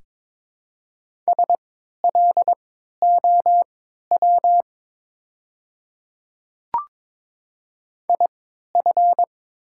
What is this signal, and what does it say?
Morse code, heard as a keyed tone, about 22 wpm, spaced as for 12 wpm, 700 Hz. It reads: SLOW IF
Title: 3x Character Spacing - Top 500 Words - 22wpm